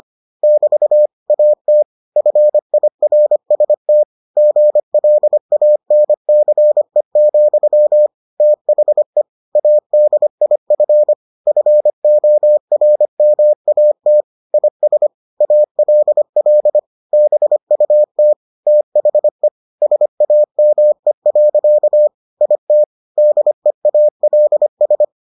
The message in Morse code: <BT> AT FIRST GLANCE, THE ADIF FORMAT IS ALL BUT THE SAME. IT DEALS

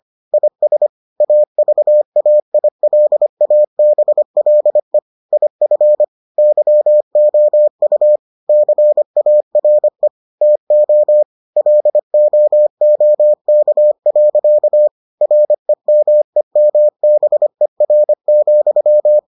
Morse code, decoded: IS AVAILABLE IF YOU CARE TO LOOK. REMEMBER,

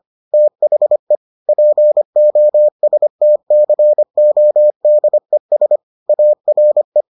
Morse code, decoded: THE POSTCODES ARE

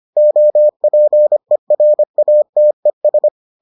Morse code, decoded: OPERATES

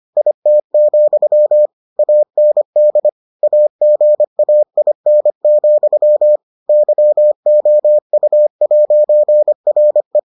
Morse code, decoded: IT, AND AGAIN, YOU'RE

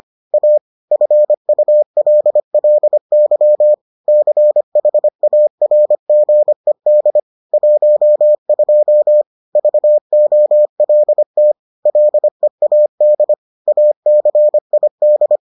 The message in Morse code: A FULLY CHARGED 12 VOLT LEAD ACID

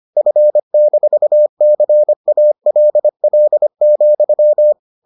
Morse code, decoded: F-CALL,

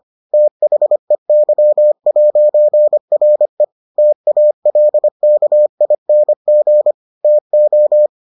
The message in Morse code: THEY'RE TALKING TO

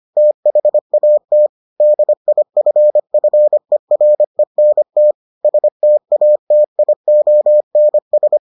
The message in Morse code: THAT DIFFERENT STATIONS

